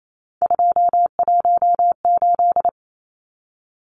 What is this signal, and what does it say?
Morse code, heard as a keyed tone, about 28 wpm, 700 Hz. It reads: 218
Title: Numbers - 3-Digit 28wpm